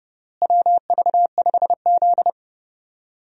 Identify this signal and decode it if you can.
Morse code, heard as a keyed tone, about 29 wpm, 700 Hz. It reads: WV5Z
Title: USA Call Signs - 4 Characters - Repeated 3x - 30wpm